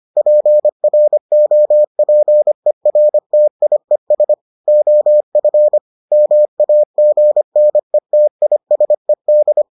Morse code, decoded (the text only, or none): PROPERTIES OF MAGNETISED